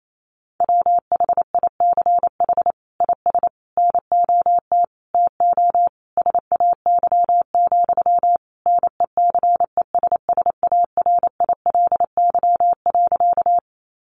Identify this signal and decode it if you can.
Morse code, heard as a keyed tone, about 28 wpm, 700 Hz. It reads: WHICH IS NOT TO SAY, NECESSARILY.
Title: Sentences from Top 1400 Words 28wpm